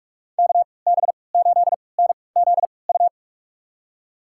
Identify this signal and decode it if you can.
Morse code, heard as a keyed tone, about 46 wpm, spaced as for 26 wpm, 700 Hz. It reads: KB8NZU
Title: USA Call Signs - 6 Characters - 3x Character Spacing - 45wpm